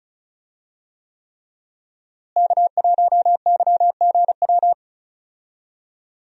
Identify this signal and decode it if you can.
Morse code, heard as a keyed tone, about 36 wpm, 700 Hz. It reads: K1YGW
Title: USA Call Signs - 5 Characters - Repeated 3x - 35wpm